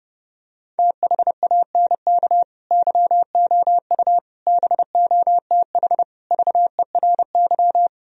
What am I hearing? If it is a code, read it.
Morse code, heard as a keyed tone, about 30 wpm, 700 Hz. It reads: THANK YOU BOTH VERY